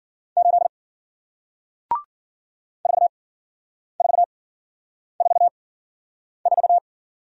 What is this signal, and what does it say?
Morse code, heard as a keyed tone, about 50 wpm, 700 Hz. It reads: 8 4 4 4 4
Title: Speed-Racing Single Letter-Number 40wpm